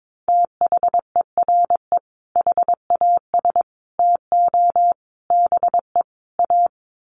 Morse code, heard as a keyed tone, about 22 wpm, 700 Hz. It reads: THERE HAS TO BE A
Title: Sets of 5 Words Encore 22wpm